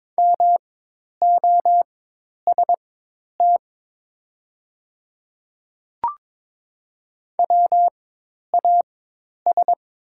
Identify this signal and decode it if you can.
Morse code, heard as a keyed tone, about 22 wpm, spaced as for 10 wpm, 700 Hz. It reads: MOST WAS
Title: 4x Character Spacing - Top 100 Words - 22wpm